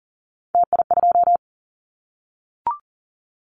Morse code, heard as a keyed tone, about 39 wpm, 700 Hz. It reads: TI2
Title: Callsign Prefixes for Top 100 DXCC - 40wpm